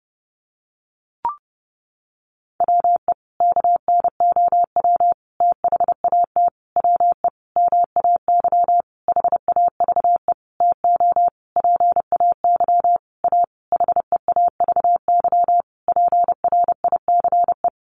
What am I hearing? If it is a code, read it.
Morse code, heard as a keyed tone, about 30 wpm, 700 Hz. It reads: WE KNOW THAT WE MAY HAVE TO PAY A HEAVY PRICE
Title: Sentences from Top 1100 Words 30wpm